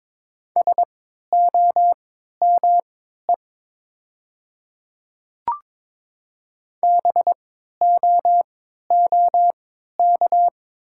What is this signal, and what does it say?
Morse code, heard as a keyed tone, about 22 wpm, spaced as for 13 wpm, 700 Hz. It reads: SOME BOOK
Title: 3x Character Spacing - Top 300 Words - 22wpm